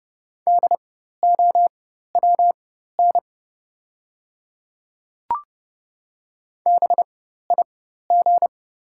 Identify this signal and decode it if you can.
Morse code, heard as a keyed tone, about 30 wpm, spaced as for 14 wpm, 700 Hz. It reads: DOWN BIG
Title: 4x Character Spacing - Top 200 Words - 30wpm